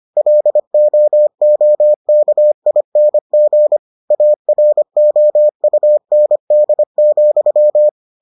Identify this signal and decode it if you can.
Morse code, heard as a keyed tone, about 25 wpm, 600 Hz. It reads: LOOKING AROUND,